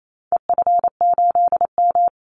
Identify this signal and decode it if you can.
Morse code, heard as a keyed tone, about 28 wpm, 700 Hz. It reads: EF8M
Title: Call Signs Repeated 3x - 28wpm